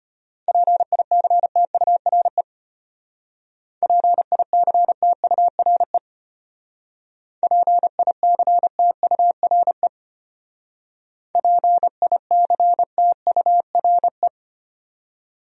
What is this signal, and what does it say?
Morse code, heard as a keyed tone, about 33 wpm, 700 Hz. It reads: PICTURE PICTURE PICTURE PICTURE